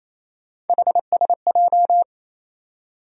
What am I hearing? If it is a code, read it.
Morse code, heard as a keyed tone, about 28 wpm, 700 Hz. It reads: HSJ
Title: USA Call Sign Postfixes - 3 Letters 28wpm